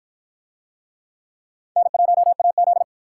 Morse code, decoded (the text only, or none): N1AZ